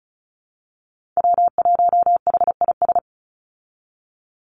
Morse code, heard as a keyed tone, about 34 wpm, 700 Hz. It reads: W1HIS
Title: Call Signs - 35wpm